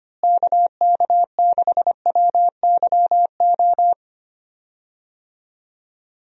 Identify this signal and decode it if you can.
Morse code, heard as a keyed tone, about 25 wpm, 700 Hz. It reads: KK6WYO
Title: USA Call Signs - 6 Characters - Repeated 2x - 25wpm